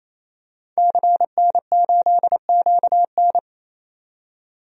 Morse code, heard as a keyed tone, about 28 wpm, 700 Hz. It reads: CN8QN
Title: DXCC-100 Call Signs - 5 Characters - 28wpm